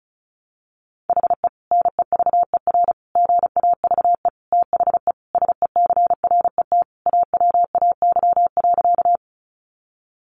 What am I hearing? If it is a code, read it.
Morse code, heard as a keyed tone, about 34 wpm, 700 Hz. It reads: HE NEVER GAVE THE SECRET AWAY.
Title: Sentences from Top 1200 Words 35wpm